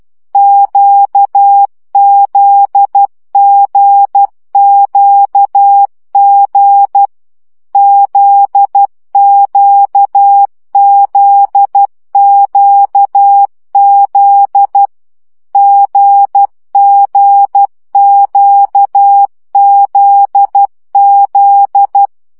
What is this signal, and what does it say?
Morse code, heard as a keyed tone, about 12 wpm, 800 Hz. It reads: QZGQG ZQZQZ GGQZZ
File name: CW gyakorlas g z q 10 baud